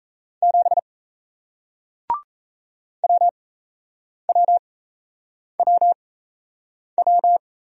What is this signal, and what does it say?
Morse code, heard as a keyed tone, about 38 wpm, spaced as for 8 wpm, 700 Hz. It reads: 7 WWWW